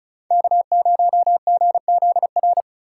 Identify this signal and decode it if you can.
Morse code, heard as a keyed tone, about 36 wpm, 700 Hz. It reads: K0GZR
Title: USA Call Signs - 5 Characters - 4x Character Spacing - 35wpm